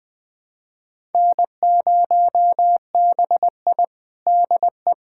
Morse code, heard as a keyed tone, about 20 wpm, 700 Hz. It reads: N0BI DE
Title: USA Callsign DE Callsign - 4 Characters - 20wpm